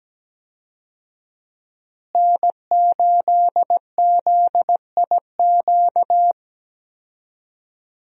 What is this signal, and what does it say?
Morse code, heard as a keyed tone, about 17 wpm, 700 Hz. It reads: N8ZIQ
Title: USA Call Signs - 5 Characters - Repeated 2x - 17wpm